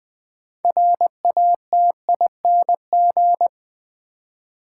Morse code, heard as a keyed tone, about 20 wpm, 700 Hz. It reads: RATING
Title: Build A Word - 2 Letters At A Time - 6-Letter Words - 20wpm